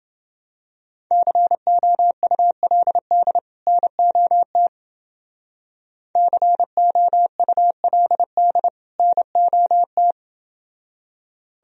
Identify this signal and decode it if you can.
Morse code, heard as a keyed tone, about 30 wpm, 700 Hz. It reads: COULD NOT COULD NOT